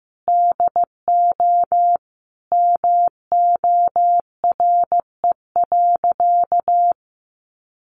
Morse code, heard as a keyed tone, about 15 wpm, 700 Hz. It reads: DO MORE.
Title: Sentences from Top 1400 Words 15wpm